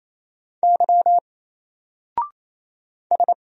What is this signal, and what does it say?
Morse code, heard as a keyed tone, about 27 wpm, 700 Hz. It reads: Y S
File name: Single-Letter-Number-28wpm